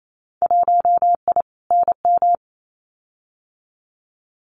Morse code, heard as a keyed tone, about 28 wpm, 700 Hz. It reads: 1I NM
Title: Winter Field Day Exchanges Encore 28wpm